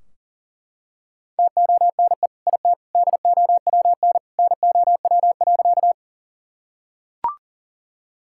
Morse code, heard as a keyed tone, about 40 wpm, 700 Hz. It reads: TONE IT DOWN NOW.
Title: Sentences from Top 1600 Words 40wpm